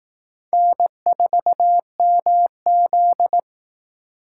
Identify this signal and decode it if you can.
Morse code, heard as a keyed tone, about 18 wpm, 700 Hz. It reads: N4MZ
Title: Speed Racing - USA 1x2 Call Signs 15wpm